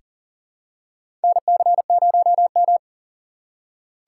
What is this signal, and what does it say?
Morse code, heard as a keyed tone, about 40 wpm, 700 Hz. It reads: NC0M
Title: USA Call Signs - 4 Characters - 2x Character Spacing - 40wpm